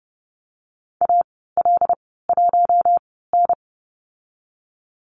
Morse code, heard as a keyed tone, about 30 wpm, spaced as for 17 wpm, 700 Hz. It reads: AL1N
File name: course / 30:10wpm / 116 Lesson 19g - Letter w - Call Signs 30wpm